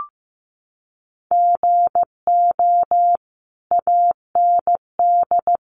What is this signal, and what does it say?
Morse code, heard as a keyed tone, about 15 wpm, 700 Hz. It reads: GO AND
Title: Sentences from Top 500 Words 15wpm